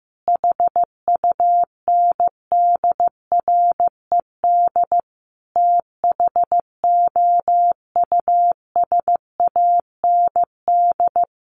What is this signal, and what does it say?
Morse code, heard as a keyed tone, about 15 wpm, 700 Hz. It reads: HUNDRED THOUSAND
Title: Sentences from Top 1100 Words 15wpm